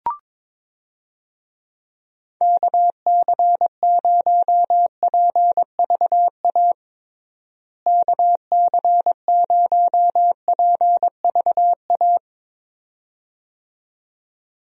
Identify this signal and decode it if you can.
Morse code, heard as a keyed tone, about 22 wpm, 700 Hz. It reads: KC0PVA KC0PVA
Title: USA Call Signs - 6 Characters - Repeated 2x - 22wpm